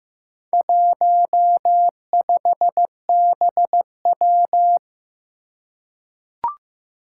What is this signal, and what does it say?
Morse code, heard as a keyed tone, about 15 wpm, 700 Hz. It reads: 15BW
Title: Four Challenging Letter-Number Combinations 15wpm